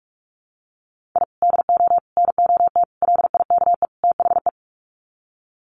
Morse code, heard as a keyed tone, about 44 wpm, 700 Hz. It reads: I DO NOT LIKE THE
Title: Sets of 5 Words Encore 45wpm